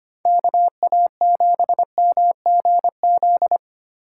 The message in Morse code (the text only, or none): KA7MGZ